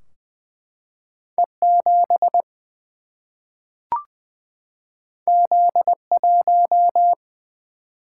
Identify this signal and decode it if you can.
Morse code, heard as a keyed tone, about 20 wpm, 700 Hz. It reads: E 7 Z 1
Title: Single-Letter Single-Number 20wpm